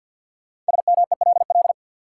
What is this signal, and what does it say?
Morse code, heard as a keyed tone, about 52 wpm, 700 Hz. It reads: SMELL
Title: Five Letter Words - 50wpm